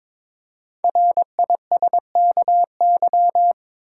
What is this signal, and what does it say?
Morse code, heard as a keyed tone, about 22 wpm, 700 Hz. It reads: RISKY
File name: 5 Letter Words - First 3 Letters Given - 22wpm